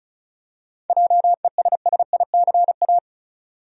Morse code, heard as a keyed tone, about 34 wpm, 700 Hz. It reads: JESSICA
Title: USA Common Names - 35wpm